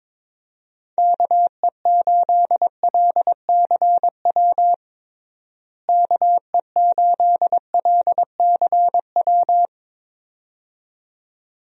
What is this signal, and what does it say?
Morse code, heard as a keyed tone, about 22 wpm, 700 Hz. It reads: KE8LCW KE8LCW
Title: USA Call Signs - 6 Characters - Repeated 2x - 22wpm